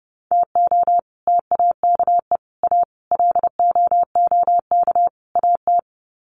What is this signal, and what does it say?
Morse code, heard as a keyed tone, about 30 wpm, 700 Hz. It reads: TO TAKE A LOOK AT
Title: Sets of 5 Words 30wpm